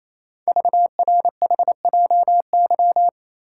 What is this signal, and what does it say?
Morse code, heard as a keyed tone, about 27 wpm, 700 Hz. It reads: VRHJY
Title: Code Groups - 5 Letter:Numbers - 28wpm